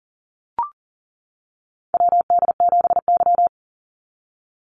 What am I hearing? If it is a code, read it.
Morse code, heard as a keyed tone, about 40 wpm, 700 Hz. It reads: WD7Y
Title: NA SOTA Chaser Callsigns 40wpm